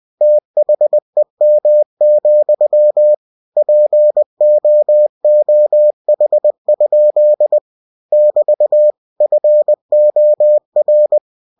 Morse code, heard as a keyed tone, about 20 wpm, 600 Hz. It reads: THEM, POOH? <BT> FOR
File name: cw-books / winnie-the-pooh / winnie-20wpm-page-0031